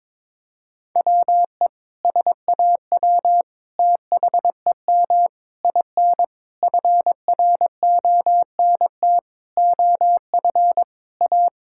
Morse code, heard as a keyed tone, about 22 wpm, 700 Hz. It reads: WE SAW THEM IN FRONT OF A
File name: practice / 22wpm / Sentences with 4-Word Phrases - 22wpm